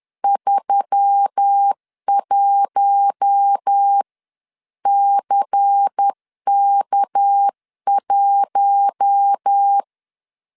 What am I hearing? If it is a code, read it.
Morse code, heard as a keyed tone, about 10 wpm, spaced as for 9 wpm, 800 Hz. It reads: 31CK1
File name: CW / DET / 1 6 1500C S C 10wpm 4252